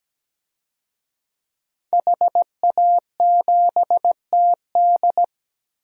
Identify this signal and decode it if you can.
Morse code, heard as a keyed tone, about 17 wpm, 700 Hz. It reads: HA7TD
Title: DXCC-100 Call Signs - 5 Characters - Repeated 3x - 17wpm